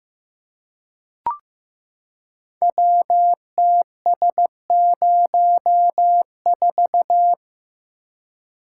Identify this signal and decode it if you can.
Morse code, heard as a keyed tone, about 15 wpm, 700 Hz. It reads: WTS04
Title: Code Groups - 5 Letter:Numbers - 15wpm